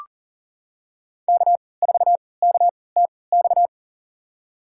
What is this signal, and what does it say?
Morse code, heard as a keyed tone, about 40 wpm, spaced as for 23 wpm, 700 Hz. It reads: K4KTX
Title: USA Call Signs - 5 Characters - 3x Character Spacing - 40wpm